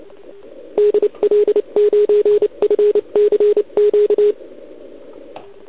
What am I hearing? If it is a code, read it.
Morse code, heard as a keyed tone, about 29 wpm, 420 Hz. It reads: DL9FCQ
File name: ARCHIV / OKDX2019 / OKOM40 KX3